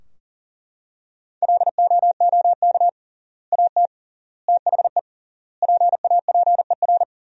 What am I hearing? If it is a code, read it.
Morse code, heard as a keyed tone, about 40 wpm, 700 Hz. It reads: LOOK AT THE PAPER
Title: 3x Word-Spacing - Sentences from Top 400 Words - 40wpm